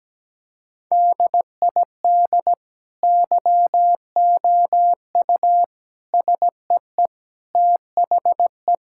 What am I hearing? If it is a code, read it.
Morse code, heard as a keyed tone, about 17 wpm, 700 Hz. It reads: DID YOU SEE THE
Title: Sentences from Top 1000 Words - Review Part 5 - 17wpm